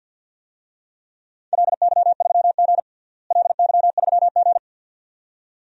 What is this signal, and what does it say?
Morse code, heard as a keyed tone, about 46 wpm, 700 Hz. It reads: LY3G LY3G
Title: DXCC-100 Call Signs - 4 Characters - Repeated 2x - 50wpm